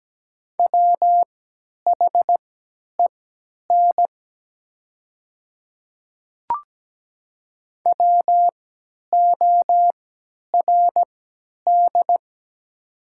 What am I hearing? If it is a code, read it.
Morse code, heard as a keyed tone, about 17 wpm, spaced as for 10 wpm, 690 Hz. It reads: WHEN WORD
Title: 3x Character Spacing - Top 100 Words - 17wpm